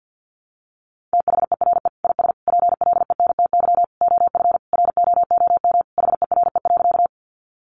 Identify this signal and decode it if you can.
Morse code, heard as a keyed tone, about 50 wpm, 700 Hz. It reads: THERE IS PLENTY OF ROOM HERE.
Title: Sentences from Top 1300 Words 50wpm